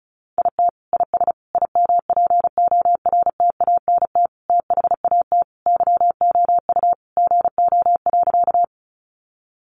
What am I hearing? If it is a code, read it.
Morse code, heard as a keyed tone, about 34 wpm, 700 Hz. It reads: IT IS IMPORTANT THAT YOU GO.
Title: Sentences from Top 600 Words 35wpm